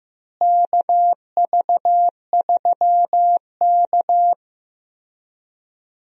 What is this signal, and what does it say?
Morse code, heard as a keyed tone, about 15 wpm, 700 Hz. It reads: KV3K
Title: USA Call Signs - 4 Characters - Repeated 2x - 15wpm